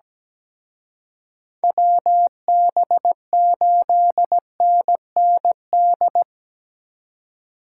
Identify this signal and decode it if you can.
Morse code, heard as a keyed tone, about 20 wpm, 700 Hz. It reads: WB8NND
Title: Speed Racing - USA Call Signs 17wpm